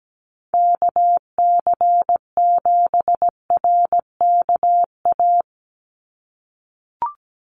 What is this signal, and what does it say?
Morse code, heard as a keyed tone, about 17 wpm, 700 Hz. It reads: KC7RKA
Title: USA Call Signs 17wpm